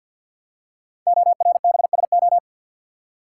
Morse code, heard as a keyed tone, about 50 wpm, 700 Hz. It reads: ORBSO
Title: Code Groups - 5 Letter:Numbers - 50wpm